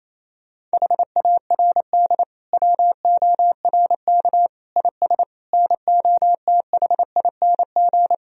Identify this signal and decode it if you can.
Morse code, heard as a keyed tone, about 28 wpm, 700 Hz. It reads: HARD WORK IS NOTHING
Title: Sentences from Top 1000 Words - Review Part 1 - 28wpm